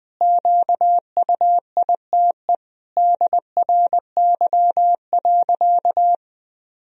Morse code, heard as a keyed tone, about 20 wpm, 700 Hz. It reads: QUITE DRY.
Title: Sentences from Top 1000 Words - Review Part 3 - 20wpm